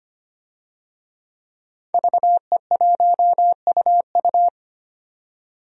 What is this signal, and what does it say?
Morse code, heard as a keyed tone, about 25 wpm, 700 Hz. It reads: VE1UU
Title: DXCC-100 Call Signs - 5 Characters - 25wpm